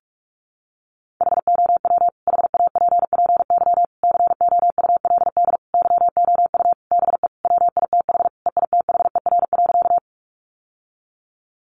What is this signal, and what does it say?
Morse code, heard as a keyed tone, about 44 wpm, 700 Hz. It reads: HOW HAPPY COULD YOU BE WITH EITHER.
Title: Sentences from Top 1000 Words 45wpm